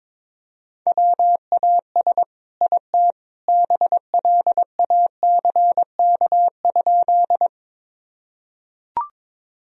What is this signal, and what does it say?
Morse code, heard as a keyed tone, about 22 wpm, 700 Hz. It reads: WAS IT BLACK?
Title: Sentences from Top 1000 Words - Review Part 2 - 22wpm